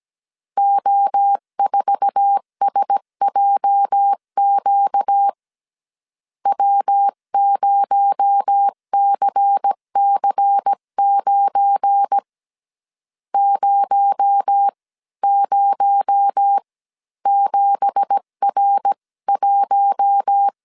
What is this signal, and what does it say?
Morse code, heard as a keyed tone, about 17 wpm, spaced as for 14 wpm, 800 Hz. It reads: O4SJQ W0CC9 007R1
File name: 1500C S B 16wpm